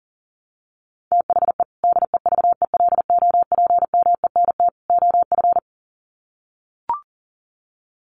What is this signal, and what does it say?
Morse code, heard as a keyed tone, about 40 wpm, 700 Hz. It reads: THE DEVELOPMENT OF
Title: Sets of 3 Words 40wpm